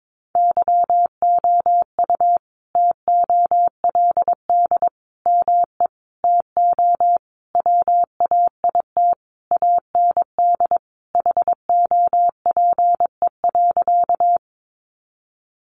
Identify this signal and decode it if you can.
Morse code, heard as a keyed tone, about 22 wpm, 700 Hz. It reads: YOU TOLD ME TO WAIT AND HOPE.
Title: Sentences from Top 1000 Words 22wpm